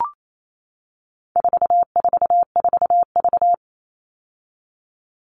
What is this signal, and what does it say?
Morse code, heard as a keyed tone, about 28 wpm, 700 Hz. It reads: 444V
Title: All Cut Numbers - 4-Digit 28wpm